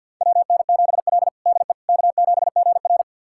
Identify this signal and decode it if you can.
Morse code, heard as a keyed tone, about 47 wpm, 700 Hz. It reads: WN7L DE K7GR
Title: USA Callsign DE Callsign - 4 Characters - 50wpm